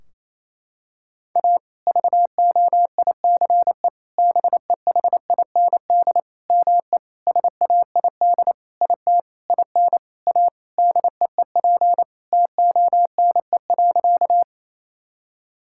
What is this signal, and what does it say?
Morse code, heard as a keyed tone, about 28 wpm, 700 Hz. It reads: A VOICE BEHIND ME SAID IT IN A DEEP TONE.
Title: Sentences from Top 1600 Words 28wpm